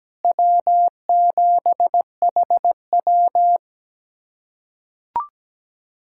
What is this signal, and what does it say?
Morse code, heard as a keyed tone, about 17 wpm, 700 Hz. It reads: W7HW
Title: USA Callsign DE Callsign - 4 Characters - 17wpm